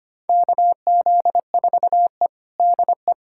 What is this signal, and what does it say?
Morse code, heard as a keyed tone, about 25 wpm, 700 Hz. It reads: KZ4E DE
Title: USA Callsign DE Callsign - 4 Characters - 25wpm